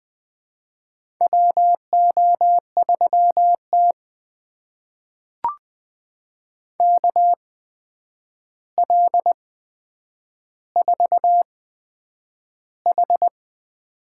Morse code, heard as a keyed tone, about 20 wpm, spaced as for 9 wpm, 700 Hz. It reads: WO3T KL4H
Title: USA Call Signs - 4 Characters - 8x Character Spacing - 20wpm